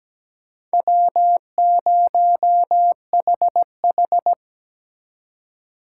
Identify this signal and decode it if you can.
Morse code, heard as a keyed tone, about 17 wpm, 700 Hz. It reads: W0HH
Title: USA Call Signs - 4 Characters - 2x Character Spacing - 17wpm